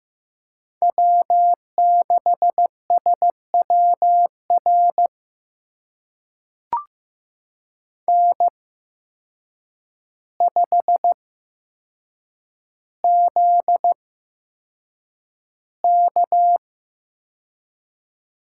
Morse code, heard as a keyed tone, about 15 wpm, spaced as for 7 wpm, 700 Hz. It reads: W6SWR N5ZK
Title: USA Call Signs - 5 Characters - 8x Character Spacing - 15wpm